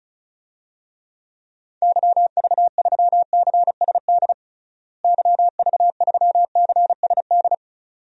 Morse code, heard as a keyed tone, about 34 wpm, 700 Hz. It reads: YV3CSD YV3CSD
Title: DXCC-100 Call Signs - 6 Characters - Repeated 2x - 35wpm